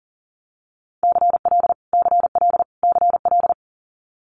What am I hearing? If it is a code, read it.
Morse code, heard as a keyed tone, about 40 wpm, 700 Hz. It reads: CL CL CL
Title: Multi-letter Phonemes Repeated 3x 40wpm